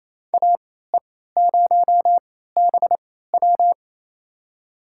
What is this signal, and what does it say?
Morse code, heard as a keyed tone, about 28 wpm, spaced as for 16 wpm, 700 Hz. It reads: AE0BW
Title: USA Call Signs - 3x Character Spacing - 28wpm